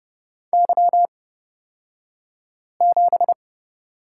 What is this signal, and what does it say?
Morse code, heard as a keyed tone, about 31 wpm, spaced as for 5 wpm, 700 Hz. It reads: Y7
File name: Lesson 36d - BT - Two Characters 30wpm